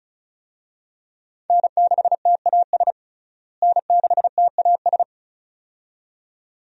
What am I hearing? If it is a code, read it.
Morse code, heard as a keyed tone, about 36 wpm, 700 Hz. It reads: N6TAS N6TAS
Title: USA Call Signs - 5 Characters - Repeated 2x - 35wpm